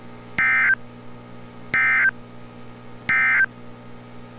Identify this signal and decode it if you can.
Emergency Alert System (SAME) tones, heard as harsh buzzing bursts. It is heard from 0.3 to 3.5 s.